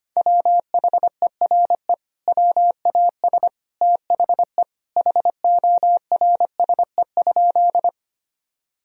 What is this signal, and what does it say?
Morse code, heard as a keyed tone, about 25 wpm, 700 Hz. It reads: WHERE WAS THE HORSE?
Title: Sentences from Top 1000 Words - Review Part 1 - 25wpm